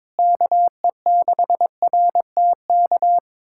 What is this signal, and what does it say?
Morse code, heard as a keyed tone, about 22 wpm, 700 Hz. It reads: KE6RTK
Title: USA Callsign DE Callsign - 6 Characters - 22wpm